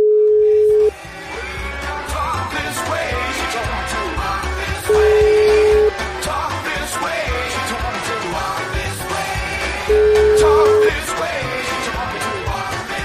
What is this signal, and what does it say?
Ringback tone (Europe, 425 Hz), with noise.